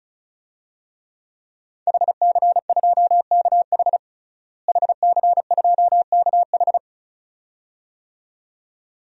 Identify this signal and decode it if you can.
Morse code, heard as a keyed tone, about 34 wpm, 700 Hz. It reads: HC2KH HC2KH
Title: DXCC-100 Call Signs - 5 Characters - Repeated 2x - 35wpm